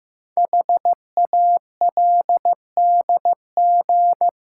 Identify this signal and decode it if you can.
Morse code, heard as a keyed tone, about 15 wpm, 700 Hz. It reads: HALDG
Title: Code Groups - 5 Letters - 15wpm